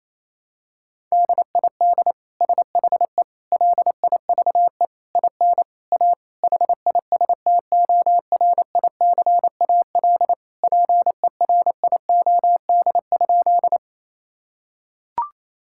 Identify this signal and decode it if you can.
Morse code, heard as a keyed tone, about 28 wpm, 700 Hz. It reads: DID SHE LIVE IN A HISTORICAL PERIOD?
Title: Sentences from Top 1000 Words - Review Part 5 - 28wpm